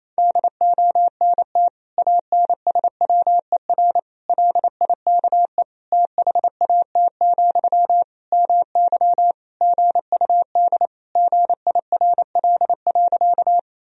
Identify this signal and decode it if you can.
Morse code, heard as a keyed tone, about 27 wpm, 700 Hz. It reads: DONT ANSWER LIKE THAT, MY GUD GIRL.
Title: Sentences from Top 400 Words and Abbreviations 28wpm